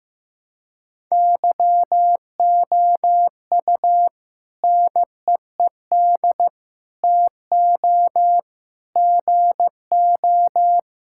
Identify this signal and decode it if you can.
Morse code, heard as a keyed tone, about 15 wpm, 700 Hz. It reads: YOU NEED TO GO
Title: Sentences from Top 1000 Words - Review Part 6 - 15wpm